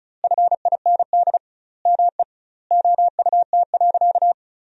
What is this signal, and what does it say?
Morse code, heard as a keyed tone, about 34 wpm, 700 Hz. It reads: FIND ME OUT.
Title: 2x Word-Spacing - Sentences from Top 500 Words - 35wpm